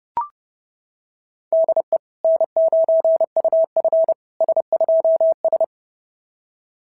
Morse code, heard as a keyed tone, about 30 wpm, 660 Hz. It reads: DE N9UF S2S
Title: SOTA QSO Practice 30wpm